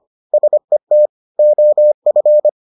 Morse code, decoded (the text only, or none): SET OF